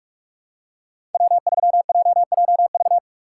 Morse code, heard as a keyed tone, about 44 wpm, 700 Hz. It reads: W3JJV
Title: USA Call Signs - 5 Characters - 6x Character Spacing - 45wpm